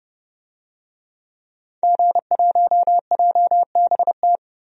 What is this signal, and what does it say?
Morse code, heard as a keyed tone, about 30 wpm, 700 Hz. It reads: G1JBT
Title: DXCC-100 Call Signs - 5 Characters - 30wpm